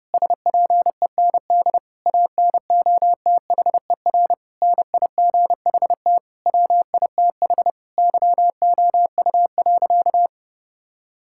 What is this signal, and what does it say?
Morse code, heard as a keyed tone, about 32 wpm, 700 Hz. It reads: SPEND ANOTHER NIGHT WITH YOU.
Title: Sentences from Top 1000 Words - Review Part 5 - 30wpm